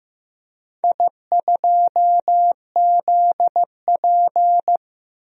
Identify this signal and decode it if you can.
Morse code, heard as a keyed tone, about 15 wpm, 700 Hz. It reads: I2ZP
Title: DXCC-100 Call Signs - 4 Characters - 15wpm